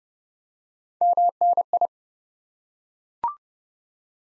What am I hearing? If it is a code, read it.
Morse code, heard as a keyed tone, about 31 wpm, 700 Hz. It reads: MNI
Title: Top 90 Abbr 30WPM